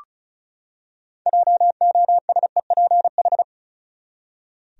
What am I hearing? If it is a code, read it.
Morse code, heard as a keyed tone, about 34 wpm, 700 Hz. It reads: JOSEPH
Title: USA Common Names - 35wpm